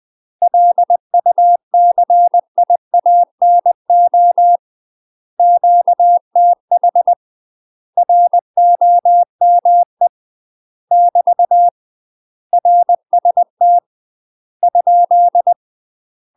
Morse code, heard as a keyed tone, about 20 wpm, 700 Hz. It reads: LUCIANO QTH ROME <BT> RST ?
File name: UK MASTER QSOs TRAINING / 20WPM / UK MASTER QSOs 0031